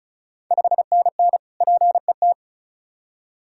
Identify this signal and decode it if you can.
Morse code, heard as a keyed tone, about 36 wpm, 690 Hz. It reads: 5NN PET
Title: Ontario QSO Party Exchange 35wpm